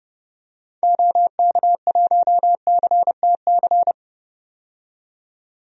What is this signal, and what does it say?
Morse code, heard as a keyed tone, about 31 wpm, 700 Hz. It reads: OK1CTC
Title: DXCC-100 Call Signs - 6 Characters - 30wpm